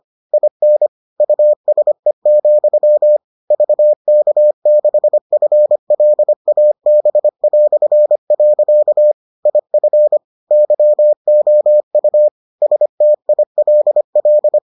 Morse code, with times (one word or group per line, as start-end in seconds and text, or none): 0.3-0.9 IN
1.2-3.2 USE,
3.5-9.1 VK6FLAB".
9.5-10.2 IF
10.5-12.3 YOU
12.6-14.6 STILL